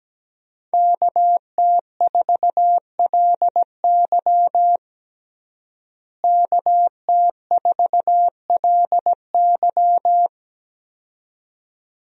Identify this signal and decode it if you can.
Morse code, heard as a keyed tone, about 17 wpm, 700 Hz. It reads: KT4LY KT4LY
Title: USA Call Signs - 5 Characters - Repeated 2x - 17wpm